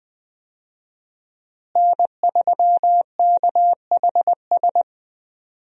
Morse code, heard as a keyed tone, about 20 wpm, 700 Hz. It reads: N3KHS
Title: USA Call Signs - 5 Characters - Repeated 3x - 20wpm